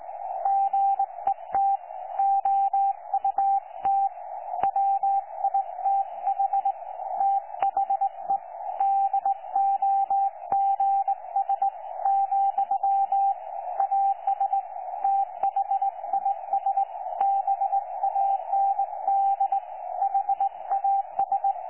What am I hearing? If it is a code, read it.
Morse code, heard as a keyed tone, about 20 wpm, 790 Hz. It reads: GET OUT WITH THE DOGS , AS THIS BAND HAS
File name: 12 39Z 18088.2kHz utah aa7fv k9tjl